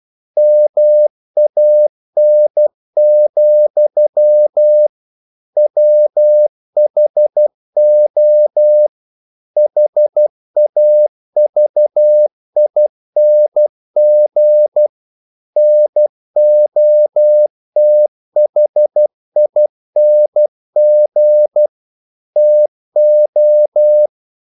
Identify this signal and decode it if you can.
Morse code, heard as a keyed tone, about 12 wpm, 600 Hz. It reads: MAN, WHO HAVING NOTHING TO
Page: English quotations in CW (telegraphy - morse)